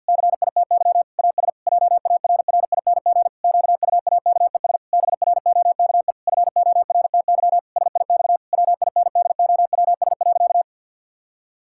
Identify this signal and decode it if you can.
Morse code, heard as a keyed tone, about 50 wpm, 700 Hz. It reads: CITY AS JARRING QUAKES BROKE FORTY SIX WINDOWS.
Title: Pangram Challenge v2 - 50wpm